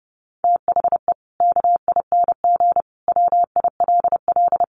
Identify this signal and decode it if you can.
Morse code, heard as a keyed tone, about 30 wpm, 700 Hz. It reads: THE KING WILL
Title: Sentences from Top 1300 Words 30wpm